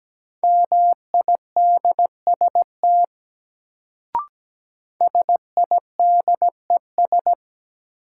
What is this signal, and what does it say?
Morse code, heard as a keyed tone, about 17 wpm, 700 Hz. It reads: MIDST SIDES
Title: Sets of 4 Words - New Word Drill - 17wpm